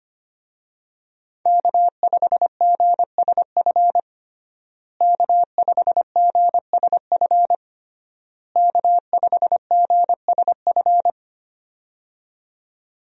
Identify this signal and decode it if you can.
Morse code, heard as a keyed tone, about 25 wpm, 700 Hz. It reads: K5GSF K5GSF K5GSF
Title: USA Call Signs - 5 Characters - Repeated 3x - 25wpm